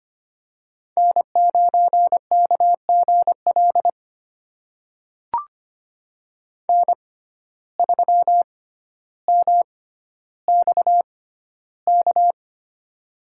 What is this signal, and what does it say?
Morse code, heard as a keyed tone, about 25 wpm, spaced as for 13 wpm, 700 Hz. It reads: N9KGL N3MXK
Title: USA Call Signs - 5 Characters - 6x Character Spacing - 25wpm